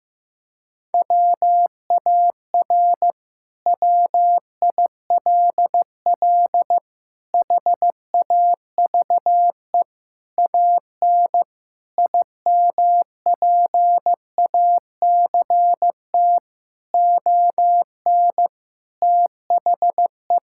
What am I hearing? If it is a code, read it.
Morse code, heard as a keyed tone, about 15 wpm, 700 Hz. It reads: WAR WILL HAVE AN IMPACT ON THE
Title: Sentences from Top 1000 Words - Review Part 6 - 15wpm